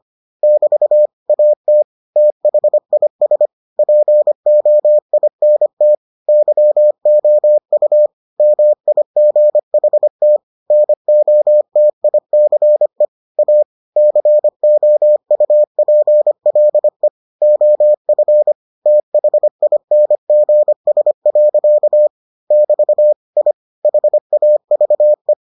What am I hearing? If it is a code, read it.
Morse code, heard as a keyed tone, about 25 wpm, 600 Hz. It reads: <BT> AT THIS POINT YOU MIGHT NOTICE A COUPLE OF THINGS. <BT> I HAVE